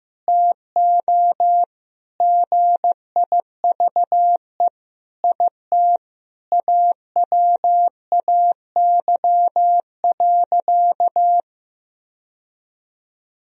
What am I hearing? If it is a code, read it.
Morse code, heard as a keyed tone, about 15 wpm, 700 Hz. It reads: TO GIVE IT AWAY.
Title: Sentences from Top 1000 Words - Review Part 1 - 15wpm